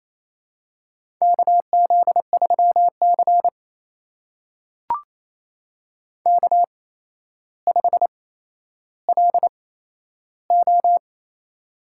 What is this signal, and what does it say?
Morse code, heard as a keyed tone, about 28 wpm, spaced as for 12 wpm, 700 Hz. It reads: KZ3C K5LO
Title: USA Call Signs - 4 Characters - 8x Character Spacing - 28wpm